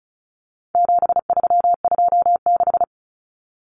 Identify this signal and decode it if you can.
Morse code, heard as a keyed tone, about 34 wpm, 700 Hz. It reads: 7326
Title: Common Cut Numbers - 4-Digit 35wpm